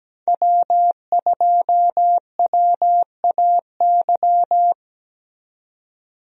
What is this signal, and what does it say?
Morse code, heard as a keyed tone, about 17 wpm, 700 Hz. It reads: W2WAY
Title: USA Call Signs - 5 Characters - 17wpm